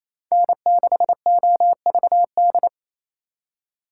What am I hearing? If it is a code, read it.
Morse code, heard as a keyed tone, about 37 wpm, 700 Hz. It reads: N6OVD